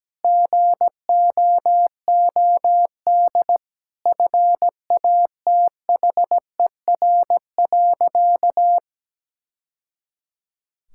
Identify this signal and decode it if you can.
Morse code, heard as a keyed tone, about 17 wpm, 700 Hz. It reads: GOOD FATHER.
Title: Sentences from Top 1000 Words - Review Part 5 - 17wpm